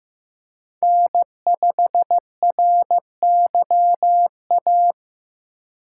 Morse code, heard as a keyed tone, about 15 wpm, 700 Hz. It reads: N5RYA
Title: USA Call Signs - 5 Characters - 3x Character Spacing - 15wpm